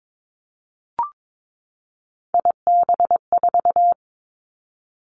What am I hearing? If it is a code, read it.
Morse code, heard as a keyed tone, about 22 wpm, 700 Hz. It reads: IB4
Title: Three Challenging Letter-Number Combinations 22wpm